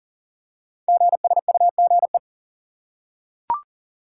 Morse code, heard as a keyed tone, about 39 wpm, 690 Hz. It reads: GSUGE
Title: Code Groups - 5 Letters - 40wpm